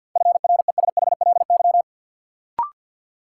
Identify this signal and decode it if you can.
Morse code, heard as a keyed tone, about 48 wpm, 700 Hz. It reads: FRESHLY
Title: Sets of 5 Words Encore - New Word Drill - 50wpm